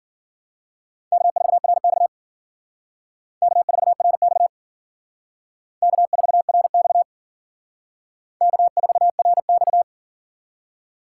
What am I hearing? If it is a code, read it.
Morse code, heard as a keyed tone, about 50 wpm, 700 Hz. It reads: K4RX K4RX K4RX K4RX
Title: Speed Racing - USA Call Signs 40wpm